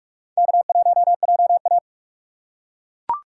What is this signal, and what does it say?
Morse code, heard as a keyed tone, about 41 wpm, 690 Hz. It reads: K1JA
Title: USA Call Signs - 3x Character Spacing - 45wpm